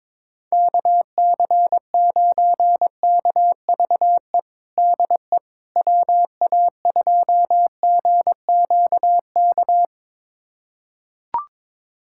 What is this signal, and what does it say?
Morse code, heard as a keyed tone, about 22 wpm, 700 Hz. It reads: KC9KVE DE WA2GQK
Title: USA Callsign DE Callsign - 6 Characters - 22wpm